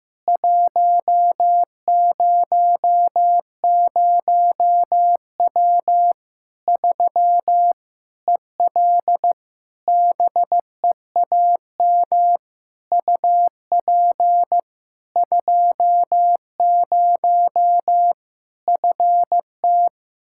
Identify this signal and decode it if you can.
Morse code, heard as a keyed tone, about 15 wpm, 700 Hz. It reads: 100W 3 EL BEAM UP 20 FT
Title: Practice QSO - Exchange 2 - Info - 15wpm